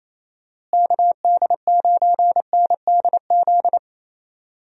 Morse code, heard as a keyed tone, about 27 wpm, 700 Hz. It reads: KD9NDZ
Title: USA Call Signs - 6 Characters - Repeated 3x - 28wpm